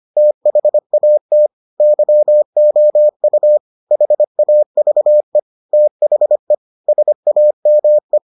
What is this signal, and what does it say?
Morse code, heard as a keyed tone, about 25 wpm, 600 Hz. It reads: THAT YOU HAVE THE SAME